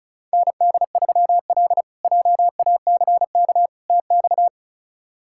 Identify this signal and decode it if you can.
Morse code, heard as a keyed tone, about 36 wpm, 700 Hz. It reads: ND3L JACK TX
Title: K1USN SST QSO Practice - 35wpm